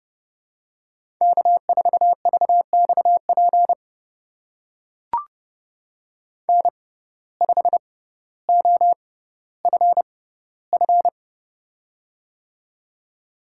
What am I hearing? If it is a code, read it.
Morse code, heard as a keyed tone, about 29 wpm, spaced as for 15 wpm, 700 Hz. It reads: K4VXP N5OFF
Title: USA Call Signs - 5 Characters - 6x Character Spacing - 30wpm